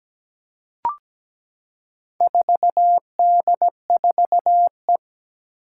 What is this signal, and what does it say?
Morse code, heard as a keyed tone, about 17 wpm, 700 Hz. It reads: 4D4E
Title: Four Challenging Letter-Number Combinations 17wpm